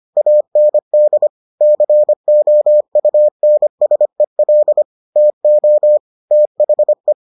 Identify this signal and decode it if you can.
Morse code, heard as a keyed tone, about 25 wpm, 600 Hz. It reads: AND COUNSEL TO THE